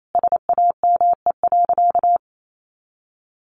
Morse code, heard as a keyed tone, about 28 wpm, 700 Hz. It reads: SAME.
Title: Sentences from Top 1400 Words 28wpm